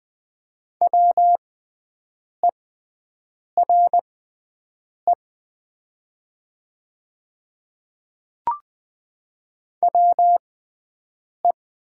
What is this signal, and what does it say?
Morse code, heard as a keyed tone, about 20 wpm, spaced as for 7 wpm, 700 Hz. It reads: WERE WE
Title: 6x Character Spacing - Top 200 Words - 20wpm